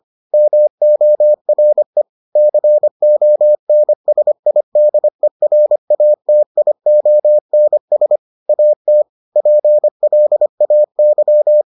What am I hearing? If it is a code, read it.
Morse code, heard as a keyed tone, about 25 wpm, 600 Hz. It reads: MORE CONSIDERATIONS AT PLAY